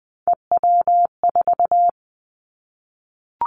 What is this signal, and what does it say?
Morse code, heard as a keyed tone, about 20 wpm, 700 Hz. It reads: EW4
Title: Callsign Prefixes for Top 50 DXCC - 20wpm